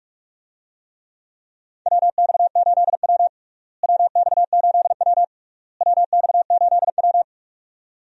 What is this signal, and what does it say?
Morse code, heard as a keyed tone, about 46 wpm, 700 Hz. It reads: WX8W WX8W WX8W
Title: USA Call Signs - 4 Characters - Repeated 3x - 45wpm